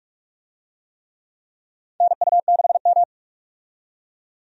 Morse code, heard as a keyed tone, about 47 wpm, 700 Hz. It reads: NU6M
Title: USA Call Signs - 4 Characters - 45wpm